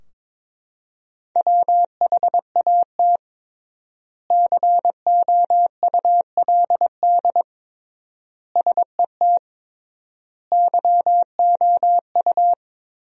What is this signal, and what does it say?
Morse code, heard as a keyed tone, about 22 wpm, 700 Hz. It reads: WHAT COULD SET YOU
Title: 3x Word-Spacing - Sentences from Top 200 Words - 22wpm